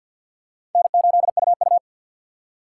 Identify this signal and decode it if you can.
Morse code, heard as a keyed tone, about 46 wpm, 700 Hz. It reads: N8UU
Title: USA Call Signs - 4 Characters - 3x Character Spacing - 50wpm